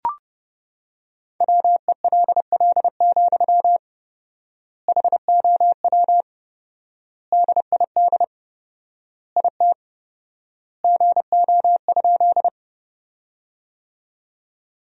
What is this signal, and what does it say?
Morse code, heard as a keyed tone, about 30 wpm, 700 Hz. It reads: WELL, HOW DID IT GO?
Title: 4x Word-Spacing - Sentences from Top 200 Words - 30wpm